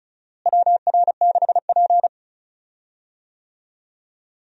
Morse code, heard as a keyed tone, about 35 wpm, 700 Hz. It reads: WR6P
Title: USA Call Signs - 4 Characters - Repeated 2x - 35wpm